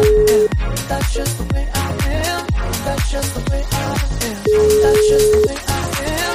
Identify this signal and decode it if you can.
Ringback tone (Europe, 425 Hz), with noise.